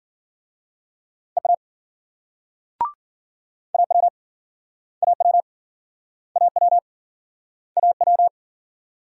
Morse code, heard as a keyed tone, about 51 wpm, 700 Hz. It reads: EA AW AW AW AW